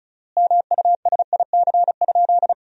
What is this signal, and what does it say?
Morse code, heard as a keyed tone, about 36 wpm, 700 Hz. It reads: MUSIC?
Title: 4x Word-Spacing - Sentences from Top 300 Words - 35wpm